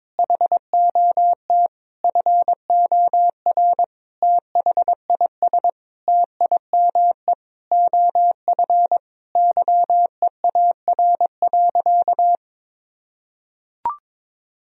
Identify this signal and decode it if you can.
Morse code, heard as a keyed tone, about 22 wpm, 700 Hz. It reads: HOT FOR THIS TIME OF YEAR.
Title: Sentences from Top 1000 Words - Review Part 3 - 22wpm